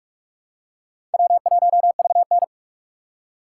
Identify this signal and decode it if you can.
Morse code, heard as a keyed tone, about 43 wpm, 700 Hz. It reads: W1VN
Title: USA Call Signs - 4 Characters - 2x Character Spacing - 45wpm